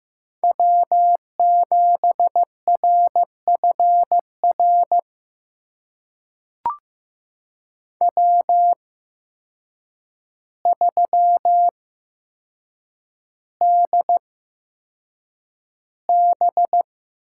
Morse code, heard as a keyed tone, about 15 wpm, spaced as for 7 wpm, 700 Hz. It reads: W7RFR W3DB
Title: USA Call Signs - 8x Character Spacing - 15wpm